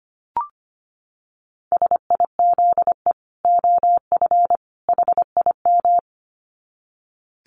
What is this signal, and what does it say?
Morse code, heard as a keyed tone, about 25 wpm, 700 Hz. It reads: SIZE OF HIM
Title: Dit Challenge 25wpm